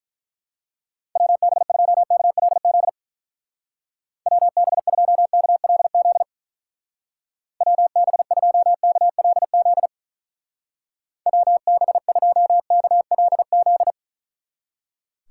Morse code, heard as a keyed tone, about 42 wpm, 700 Hz. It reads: WB2KLZ WB2KLZ WB2KLZ WB2KLZ
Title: Speed Racing - USA 2x3 Call Signs 35wpm